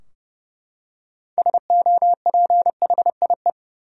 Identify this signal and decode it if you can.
Morse code, heard as a keyed tone, about 30 wpm, 700 Hz. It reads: SOPHIE
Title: USA Common Names - 30wpm